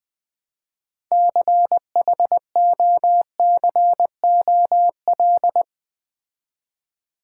Morse code, heard as a keyed tone, about 20 wpm, 700 Hz. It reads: CHOCOL
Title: Build A Word - 3 Letters At A Time - 9-Letter Words - 20wpm